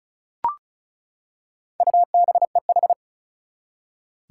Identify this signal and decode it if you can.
Morse code, heard as a keyed tone, about 36 wpm, 700 Hz. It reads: UBEH
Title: Four Challenging Letter-Number Combinations 35wpm